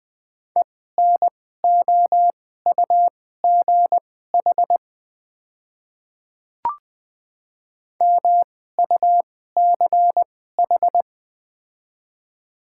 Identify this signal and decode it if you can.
Morse code, heard as a keyed tone, about 21 wpm, spaced as for 15 wpm, 700 Hz. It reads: ENOUGH MUCH
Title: 2x Character Spacing - Top 500 Words - 20wpm